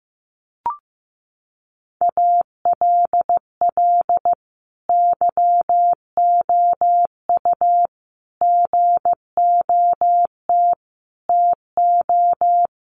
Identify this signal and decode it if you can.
Morse code, heard as a keyed tone, about 15 wpm, 700 Hz. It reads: ALL YOU GOT TO
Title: A Sound Of Thunder 15wpm